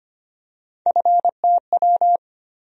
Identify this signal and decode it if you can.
Morse code, heard as a keyed tone, about 25 wpm, 700 Hz. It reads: FTW
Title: USA Call Sign Postfixes - 3 Letters 25wpm